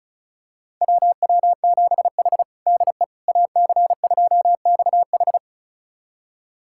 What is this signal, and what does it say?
Morse code, heard as a keyed tone, about 34 wpm, 700 Hz. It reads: WW7H DE AC2XH
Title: USA Callsign DE Callsign 35wpm